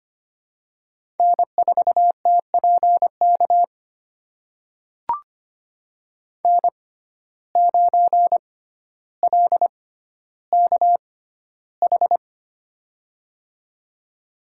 Morse code, heard as a keyed tone, about 25 wpm, spaced as for 13 wpm, 700 Hz. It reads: N4TPK N9LKH
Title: USA Call Signs - 5 Characters - 6x Character Spacing - 25wpm